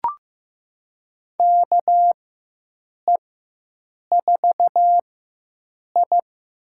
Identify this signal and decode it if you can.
Morse code, heard as a keyed tone, about 15 wpm, spaced as for 7 wpm, 700 Hz. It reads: KE4I
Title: USA Call Signs - 6 Characters - 4x Character Spacing - 15wpm